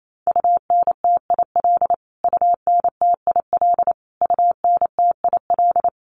Morse code, heard as a keyed tone, about 28 wpm, 700 Hz. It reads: UNTIL UNTIL UNTIL
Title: Top 500 Words Repeated 28wpm